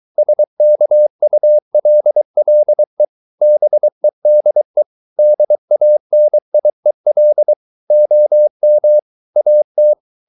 Morse code, heard as a keyed tone, about 23 wpm, 600 Hz. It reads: SKULLE BEDE DANIEL OM AT
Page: Ferie 23 wpm